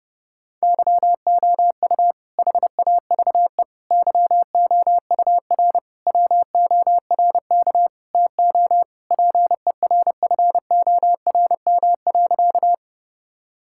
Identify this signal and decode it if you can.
Morse code, heard as a keyed tone, about 30 wpm, 700 Hz. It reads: YOU HAVE YOUR WORK TO PERFORM.
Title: Sentences from Top 1000 Words - Review Part 5 - 30wpm